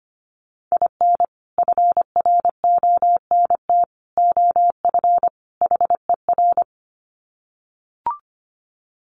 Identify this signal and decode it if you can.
Morse code, heard as a keyed tone, about 25 wpm, 700 Hz. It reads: IN FRONT OF HER
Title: Sets of 4 Words 25wpm